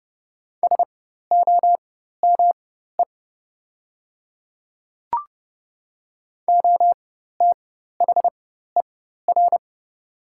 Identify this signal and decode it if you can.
Morse code, heard as a keyed tone, about 30 wpm, spaced as for 14 wpm, 700 Hz. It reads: SOME OTHER
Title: 4x Character Spacing - Top 100 Words - 30wpm